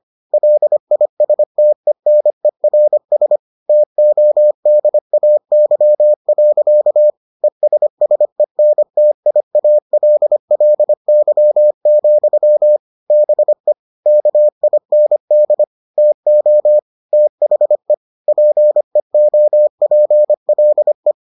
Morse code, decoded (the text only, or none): LISTENERS TODAY. ESSENTIALLY, BE KIND TO THE PEOPLE